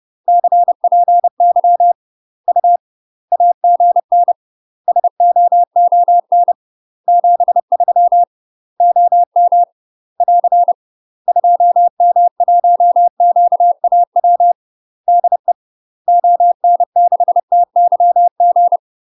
Morse code, heard as a keyed tone, about 30 wpm, 700 Hz. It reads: CPY U AGN SOON 73 OM <AR> 2M1QAW DE ON6TYG
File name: UK MASTER QSOs TRAINING / 30WPM / UK MASTER QSOs 0033